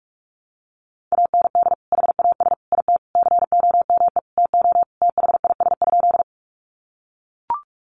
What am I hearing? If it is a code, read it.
Morse code, heard as a keyed tone, about 44 wpm, 700 Hz. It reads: AND HAS IT COME TO THIS?
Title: Sentences from Top 100 Words 45wpm